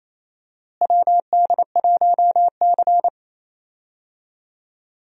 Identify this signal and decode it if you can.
Morse code, heard as a keyed tone, about 28 wpm, 700 Hz. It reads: WD1C
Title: USA Call Signs - 4 Characters - Repeated 2x - 28wpm